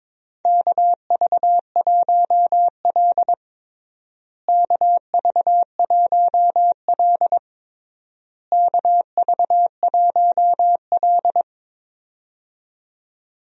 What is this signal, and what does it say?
Morse code, heard as a keyed tone, about 22 wpm, 700 Hz. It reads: KV1L KV1L KV1L
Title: USA Call Signs - 4 Characters - Repeated 3x - 22wpm